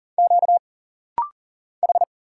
Morse code, heard as a keyed tone, about 39 wpm, 700 Hz. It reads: Q H
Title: Single Letters - Rapid-Fire 40wpm